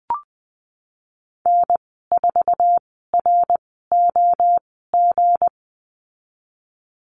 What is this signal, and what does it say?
Morse code, heard as a keyed tone, about 20 wpm, spaced as for 15 wpm, 700 Hz. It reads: N4ROG
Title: 167 Lesson 27g - Number 7 - Call Signs 20wpm